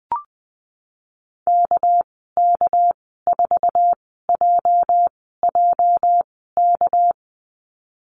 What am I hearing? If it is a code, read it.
Morse code, heard as a keyed tone, about 19 wpm, 700 Hz. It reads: K K 4 J J K
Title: 238 Lesson 38g - Letter q - Call Signs 20wpm